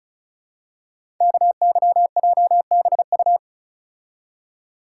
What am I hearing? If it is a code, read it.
Morse code, heard as a keyed tone, about 36 wpm, 690 Hz. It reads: KYJBU
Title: Code Groups - 5 Letters - 35wpm